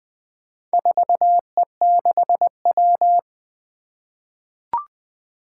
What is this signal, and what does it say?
Morse code, heard as a keyed tone, about 20 wpm, 700 Hz. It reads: 4E6W
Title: Four Letter-Number Combinations - 20wpm